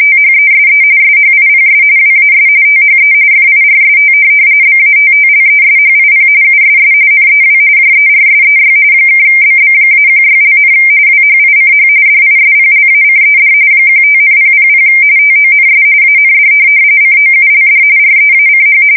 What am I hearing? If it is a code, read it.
RTTY, about 45.45 baud. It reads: THESE SHIPS WAS THE FIRST BATTLESHIP USS NEW JERSEY (BB-16), WHICH DISPLACED 15,000 TONS AND COULD REACH
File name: fsk170 / USS-New-Jersey